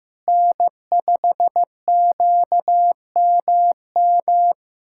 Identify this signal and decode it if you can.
Morse code, heard as a keyed tone, about 16 wpm, 700 Hz. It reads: N5QMM
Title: USA Callsign DE Callsign - 5 Characters - 15wpm